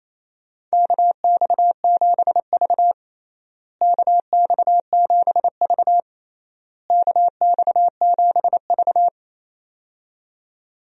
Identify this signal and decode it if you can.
Morse code, heard as a keyed tone, about 28 wpm, 700 Hz. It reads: KX7V KX7V KX7V
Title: USA Call Signs - 4 Characters - Repeated 3x - 28wpm